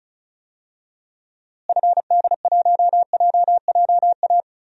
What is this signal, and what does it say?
Morse code, heard as a keyed tone, about 36 wpm, 700 Hz. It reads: FD1JJA
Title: DXCC-100 Call Signs - 6 Characters - Repeated 3x - 35wpm